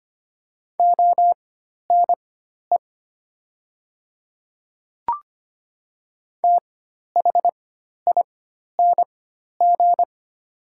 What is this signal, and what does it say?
Morse code, heard as a keyed tone, about 25 wpm, spaced as for 12 wpm, 700 Hz. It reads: ONE THING
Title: 4x Character Spacing - Top 100 Words - 25wpm